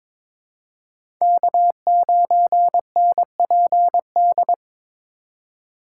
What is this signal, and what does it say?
Morse code, heard as a keyed tone, about 21 wpm, 700 Hz. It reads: K9NPD
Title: USA Call Signs - 5 Characters - 3x Character Spacing - 22wpm